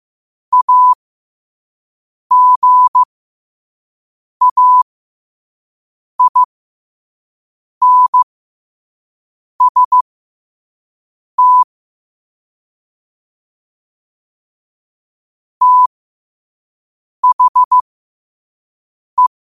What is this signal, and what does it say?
Morse code, heard as a keyed tone, about 15 wpm, spaced as for 5 wpm, 1000 Hz. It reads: AGAINST THE
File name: 5wpm